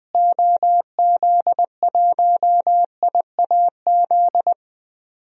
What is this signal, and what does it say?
Morse code, heard as a keyed tone, about 21 wpm, 700 Hz. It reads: OZ1IAZ
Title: DXCC-100 Call Signs - 6 Characters - Repeated 2x - 20wpm